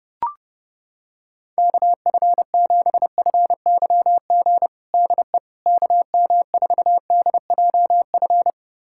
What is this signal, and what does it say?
Morse code, heard as a keyed tone, about 30 wpm, 700 Hz. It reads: KF7FYG DE KM4DJF
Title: USA Callsign DE Callsign 30wpm